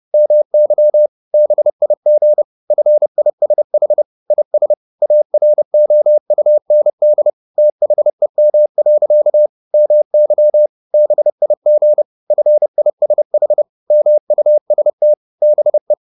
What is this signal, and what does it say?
Morse code, heard as a keyed tone, about 30 wpm, 600 Hz. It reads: MY BIG FISH IS AROUND THEM. MY BIG FISH MUST BE
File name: the old man and the sea / the old man and the sea 1 30wpm